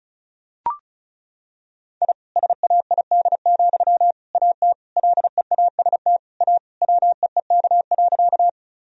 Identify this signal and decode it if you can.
Morse code, heard as a keyed tone, about 36 wpm, 700 Hz. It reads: I SAID, AT LEAST A WEEK.
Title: Sentences from Top 1000 Words - Review Part 2 - 35wpm